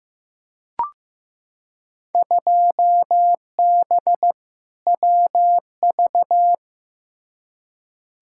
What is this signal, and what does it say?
Morse code, heard as a keyed tone, about 15 wpm, 700 Hz. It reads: 2B WV
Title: Field Day Exchanges Encore - 15wpm